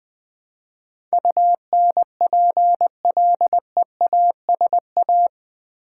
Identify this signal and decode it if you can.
Morse code, heard as a keyed tone, about 21 wpm, 700 Hz. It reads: UNPLEASA
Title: Build A Word - 4 Letters At A Time - 10-Letter Words - 20wpm